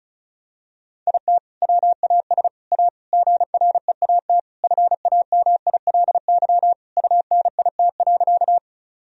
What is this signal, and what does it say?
Morse code, heard as a keyed tone, about 36 wpm, 700 Hz. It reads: IT WAS A GREAT FAMILY UNIT.
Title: Sentences from Top 1000 Words - Review Part 2 - 35wpm